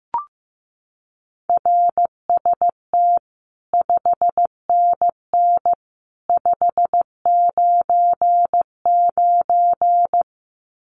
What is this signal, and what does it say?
Morse code, heard as a keyed tone, about 15 wpm, 700 Hz. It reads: RST 5NN 599
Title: Common QSO Phrases 15wpm